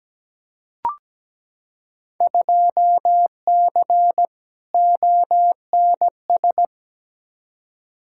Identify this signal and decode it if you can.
Morse code, heard as a keyed tone, about 17 wpm, 700 Hz. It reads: 2C ONS
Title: Field Day Exchanges Encore - 17wpm